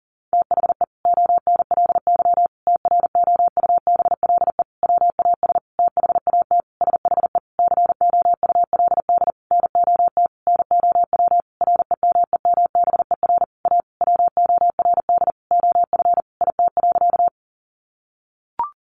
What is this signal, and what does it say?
Morse code, heard as a keyed tone, about 40 wpm, 700 Hz. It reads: THE ONLY TROUBLE WAS THAT SHE COULD NOT NOW REMEMBER A WORD OF IT.
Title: Sentences from Top 1200 Words 40wpm